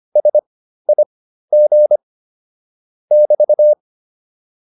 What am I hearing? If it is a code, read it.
Morse code, heard as a keyed tone, about 25 wpm, spaced as for 13 wpm, 600 Hz. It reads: SIG <BT>
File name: QSO 407 13